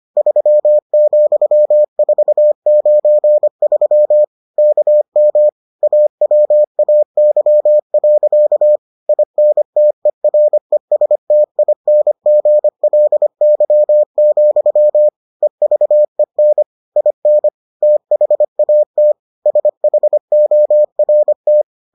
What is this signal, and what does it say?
Morse code, heard as a keyed tone, about 25 wpm, 600 Hz. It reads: 3,493 KM AWAY. INTERESTINGLY, EVEN IN THAT SHORT